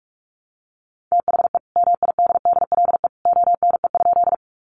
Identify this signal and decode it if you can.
Morse code, heard as a keyed tone, about 44 wpm, 700 Hz. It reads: THE MIDDLE ONE?
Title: Sentences from Top 900 Words 45wpm